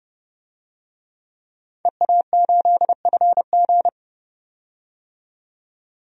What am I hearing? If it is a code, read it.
Morse code, heard as a keyed tone, about 32 wpm, 700 Hz. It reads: EA8FG
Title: DXCC-100 Call Signs - 5 Characters - 30wpm